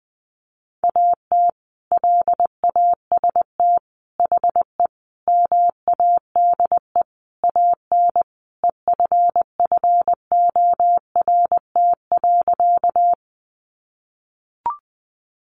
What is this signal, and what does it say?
Morse code, heard as a keyed tone, about 20 wpm, 700 Hz. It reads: AT LAST HE MADE AN EFFORT.
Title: Sentences from Top 1200 Words 20wpm